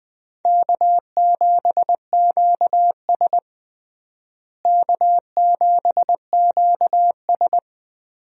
Morse code, heard as a keyed tone, about 20 wpm, 700 Hz. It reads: K7QS K7QS
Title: USA Call Signs - 4 Characters - Repeated 3x - 20wpm